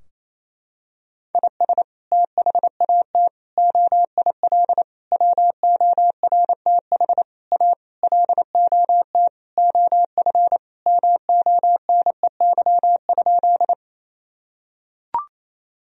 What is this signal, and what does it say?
Morse code, heard as a keyed tone, about 28 wpm, 700 Hz. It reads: IS THAT OIL WORTH A LOT OF MONEY?
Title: Sentences from Top 1000 Words - Review Part 4 - 28wpm